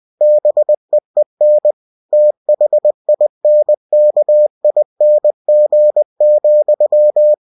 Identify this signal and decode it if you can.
Morse code, heard as a keyed tone, about 20 wpm, 600 Hz. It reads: BEEN THINKING,
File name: cw-books / winnie-the-pooh / winnie-20wpm-page-0009